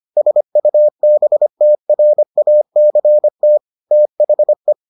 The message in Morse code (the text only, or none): SUBTRACT THE